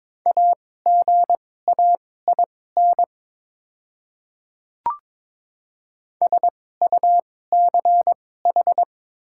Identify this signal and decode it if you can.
Morse code, heard as a keyed tone, about 22 wpm, spaced as for 16 wpm, 700 Hz. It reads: AGAIN SUCH
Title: 2x Character Spacing - Top 200 Words - 22wpm